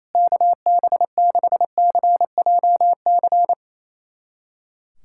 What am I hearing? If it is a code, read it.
Morse code, heard as a keyed tone, about 28 wpm, 700 Hz. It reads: KB6CJC
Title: USA Callsign DE Callsign - 6 Characters - 28wpm